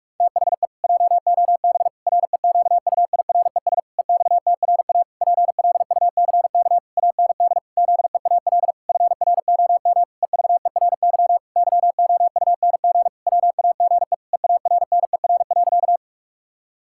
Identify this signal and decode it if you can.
Morse code, heard as a keyed tone, about 46 wpm, 700 Hz. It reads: THE JOB REQUIRES EXTRA PLUCK AND ZEAL FROM EVERY YOUNG WAGE EARNER.
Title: Pangram Challenge v2 - 45wpm